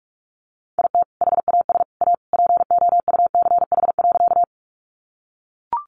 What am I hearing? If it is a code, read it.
Morse code, heard as a keyed tone, about 44 wpm, 700 Hz. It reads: IT HAS A POUCH.
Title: Easy Phrases 45wpm